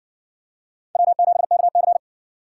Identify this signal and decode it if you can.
Morse code, heard as a keyed tone, about 53 wpm, 700 Hz. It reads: W7KC